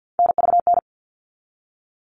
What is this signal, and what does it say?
Morse code, heard as a keyed tone, about 48 wpm, 700 Hz. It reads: NVN
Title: All Cut Numbers - 3-Digit 50wpm